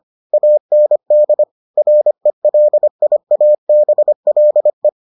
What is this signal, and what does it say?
Morse code, heard as a keyed tone, about 25 wpm, 600 Hz. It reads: AND RELIABLE